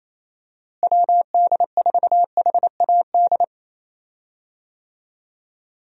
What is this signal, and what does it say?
Morse code, heard as a keyed tone, about 28 wpm, 700 Hz. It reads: WD4HAD
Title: USA Call Signs - 6 Characters - Repeated 3x - 28wpm